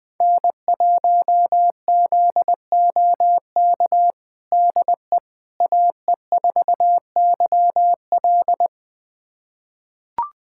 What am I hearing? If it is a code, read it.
Morse code, heard as a keyed tone, about 20 wpm, 700 Hz. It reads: N1ZOK DE AE4YL
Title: USA Callsign DE Callsign - 5 Characters - 20wpm